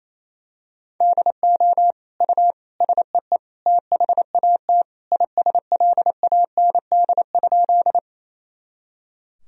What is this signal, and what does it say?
Morse code, heard as a keyed tone, about 28 wpm, 700 Hz. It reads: DO U SEE THAT ISLAND?
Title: Sentences from Top 500 Words and Abbreviations 28wpm